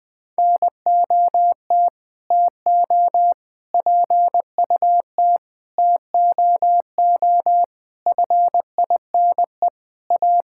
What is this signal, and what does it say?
Morse code, heard as a keyed tone, about 21 wpm, 700 Hz. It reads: NOT TO PUT TOO FINE A
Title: Sets of 8 Words - 20wpm